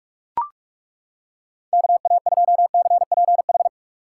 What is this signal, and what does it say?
Morse code, heard as a keyed tone, about 39 wpm, 700 Hz. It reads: KA2CPH